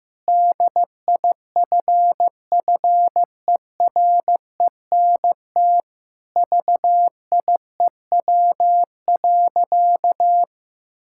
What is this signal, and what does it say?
Morse code, heard as a keyed tone, about 15 wpm, 700 Hz. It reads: DIFFERENT VIEW.
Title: Sentences from Top 1600 Words 15wpm